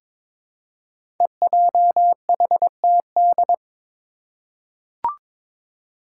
Morse code, heard as a keyed tone, about 22 wpm, 700 Hz. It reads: EJHTD
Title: Five Challenging Letter-Number Combinations 22wpm